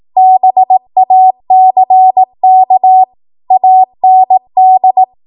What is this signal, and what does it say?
Morse code, heard as a keyed tone, about 18 wpm, 750 Hz. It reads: BACK AND
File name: Morse / 18WPM